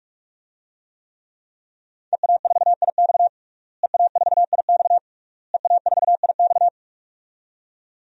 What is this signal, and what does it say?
Morse code, heard as a keyed tone, about 46 wpm, 700 Hz. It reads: EA4IX EA4IX EA4IX
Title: DXCC-100 Call Signs - 5 Characters - Repeated 3x - 45wpm